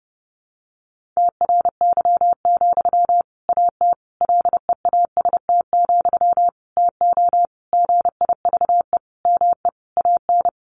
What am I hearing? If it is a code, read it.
Morse code, heard as a keyed tone, about 30 wpm, 700 Hz. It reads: TRY, AT LEAST, TO GIVE ME AN
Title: Sentences from Top 700 Words 30wpm